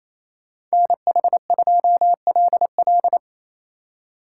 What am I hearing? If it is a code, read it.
Morse code, heard as a keyed tone, about 28 wpm, 700 Hz. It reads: NH2LL
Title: Speed Racing - USA 2x2 Call Signs 28wpm